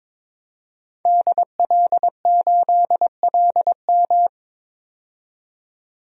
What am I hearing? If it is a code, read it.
Morse code, heard as a keyed tone, about 21 wpm, 700 Hz. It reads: DL8LM
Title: DXCC-100 Call Signs - 5 Characters - 22wpm